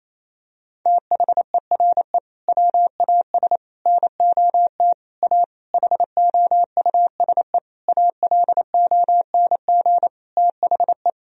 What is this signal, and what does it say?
Morse code, heard as a keyed tone, about 28 wpm, 700 Hz. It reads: THERE WAS NOT A HOUSE ALONG THE
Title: Sentences from Top 1000 Words - Review Part 2 - 28wpm